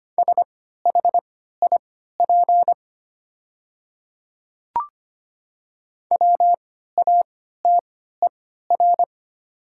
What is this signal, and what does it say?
Morse code, heard as a keyed tone, about 25 wpm, spaced as for 14 wpm, 700 Hz. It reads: SHIP WATER
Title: 3x Character Spacing - Top 500 Words - 25wpm